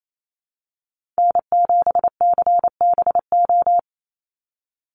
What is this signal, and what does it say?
Morse code, heard as a keyed tone, about 28 wpm, 700 Hz. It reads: N7CBO
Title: USA Call Signs 28wpm